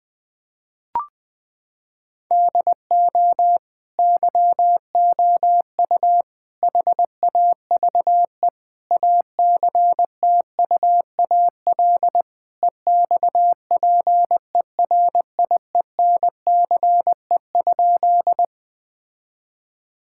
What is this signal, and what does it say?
Morse code, heard as a keyed tone, about 20 wpm, 700 Hz. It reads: DO YOU HAVE ACTUAL EXPERIENCE?
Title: Sentences from Top 1000 Words - Review Part 5 - 20wpm